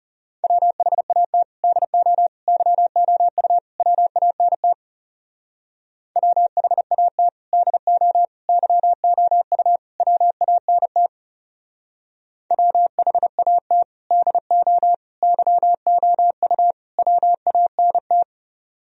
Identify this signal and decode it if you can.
Morse code, heard as a keyed tone, about 37 wpm, 700 Hz. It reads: WHAT DO YOU WANT WHAT DO YOU WANT WHAT DO YOU WANT
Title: Speed-Racing - Sets of 4 Words 30wpm